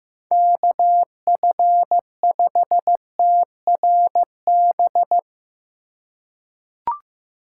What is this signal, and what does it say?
Morse code, heard as a keyed tone, about 15 wpm, 700 Hz. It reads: KF5TRB
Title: USA Call Signs - 6 Characters - 4x Character Spacing - 15wpm